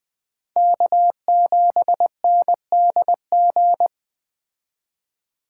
Morse code, heard as a keyed tone, about 20 wpm, 700 Hz. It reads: K7NDG